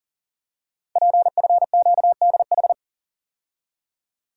Morse code, heard as a keyed tone, about 40 wpm, 700 Hz. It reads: PFQDH
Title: Code Groups - 5 Letters - 40wpm